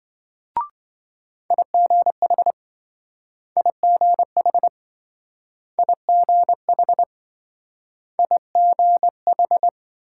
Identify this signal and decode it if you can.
Morse code, heard as a keyed tone, about 27 wpm, 700 Hz. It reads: IGH IGH IGH IGH